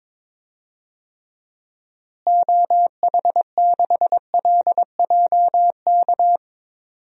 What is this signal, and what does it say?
Morse code, heard as a keyed tone, about 22 wpm, 700 Hz. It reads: OH6LJK
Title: DXCC-100 Call Signs - 6 Characters - Repeated 2x - 22wpm